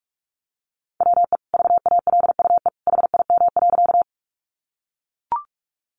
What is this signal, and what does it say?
Morse code, heard as a keyed tone, about 44 wpm, 700 Hz. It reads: WE VALUE HIM.
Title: Sentences from Top 600 Words 45wpm